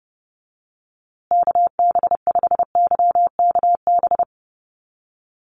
Morse code, heard as a keyed tone, about 30 wpm, 700 Hz. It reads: KB5YKB
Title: USA Call Signs Repeated 2x 30wpm